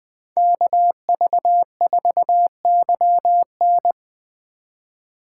Morse code, heard as a keyed tone, about 20 wpm, 700 Hz. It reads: KV4YN
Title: Speed Racing - USA 2x2 Call Signs 17wpm